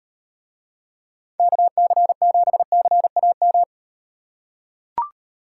Morse code, heard as a keyed tone, about 38 wpm, 700 Hz. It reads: KC7CAM